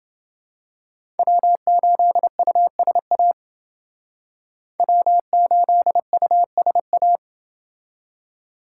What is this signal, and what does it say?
Morse code, heard as a keyed tone, about 29 wpm, 700 Hz. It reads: W8USA W8USA